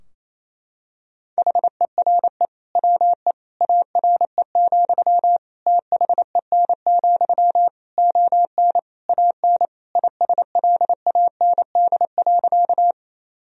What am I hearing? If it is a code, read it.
Morse code, heard as a keyed tone, about 28 wpm, 700 Hz. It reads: HERE WE ARE, THEN, ON AN ISLAND.
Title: Sentences from Top 1000 Words - Review Part 3 - 28wpm